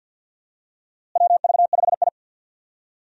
Morse code, heard as a keyed tone, about 50 wpm, 700 Hz. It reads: WV5I
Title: USA Call Signs - 4 Characters - Repeated 2x - 50wpm